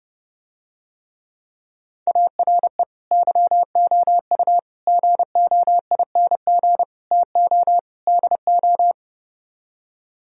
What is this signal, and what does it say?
Morse code, heard as a keyed tone, about 32 wpm, 700 Hz. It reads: ARE YOU GOING TO DO